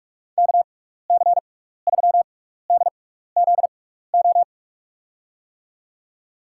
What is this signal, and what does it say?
Morse code, heard as a keyed tone, about 43 wpm, spaced as for 15 wpm, 700 Hz. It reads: KC3DZO
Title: USA Call Signs - 6 Characters - 6x Character Spacing - 45wpm